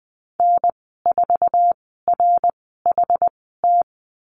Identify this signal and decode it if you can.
Morse code, heard as a keyed tone, about 20 wpm, spaced as for 15 wpm, 700 Hz. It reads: N4RHT
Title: 076 Lesson 13g - Letter L - Call Signs 20wpm